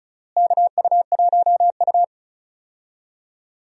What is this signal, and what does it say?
Morse code, heard as a keyed tone, about 36 wpm, 700 Hz. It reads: KU1U
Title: USA Call Signs - 4 Characters - Repeated 2x - 35wpm